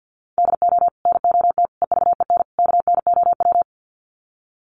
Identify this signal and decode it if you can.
Morse code, heard as a keyed tone, about 50 wpm, 700 Hz. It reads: DO NOT EVEN KNOW
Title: Sets of 4 Words 50wpm